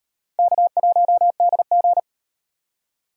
Morse code, heard as a keyed tone, about 37 wpm, 700 Hz. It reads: K1DG